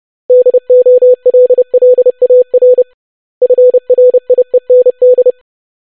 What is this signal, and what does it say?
Morse code, heard as a keyed tone, about 30 wpm, 500 Hz. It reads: DOLLAR FRIEND
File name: basic500-25wpm